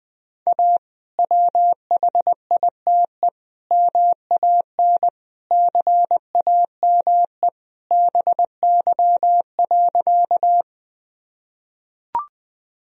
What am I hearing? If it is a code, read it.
Morse code, heard as a keyed tone, about 21 wpm, 700 Hz. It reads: A WHITE MAN CAME BY.
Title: Sentences from Top 1000 Words - Review Part 1 - 20wpm